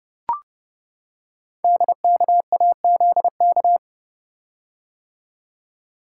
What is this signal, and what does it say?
Morse code, heard as a keyed tone, about 30 wpm, 700 Hz. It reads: DKAZK
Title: Code Groups - 5 Letters - 30wpm